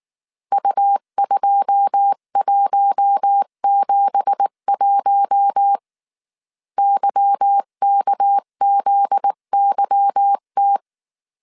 Morse code, heard as a keyed tone, about 19 wpm, 800 Hz. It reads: U2171 YKZYT
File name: CW / DET / 1 9 1500C S B 18wpm 2551